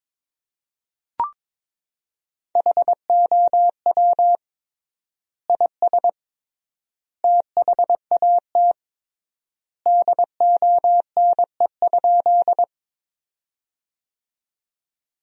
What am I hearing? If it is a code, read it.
Morse code, heard as a keyed tone, about 21 wpm, 700 Hz. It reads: HOW IS THAT DONE?
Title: 3x Word-Spacing - Sentences from Top 500 Words - 22wpm